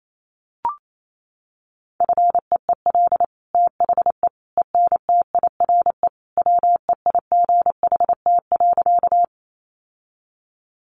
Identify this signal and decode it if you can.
Morse code, heard as a keyed tone, about 28 wpm, 700 Hz. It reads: FEEL THE ENTIRE WEIGHT.
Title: Sentences from Top 800 Words 28wpm